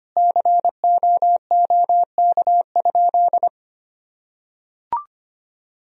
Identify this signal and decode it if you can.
Morse code, heard as a keyed tone, about 25 wpm, 700 Hz. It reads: COOK?
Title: Sentences from Top 1000 Words - Review Part 4 - 25wpm